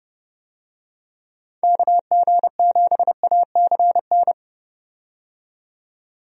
Morse code, heard as a keyed tone, about 30 wpm, 700 Hz. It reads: KG7ACN
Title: USA Call Signs - 6 Characters - 30wpm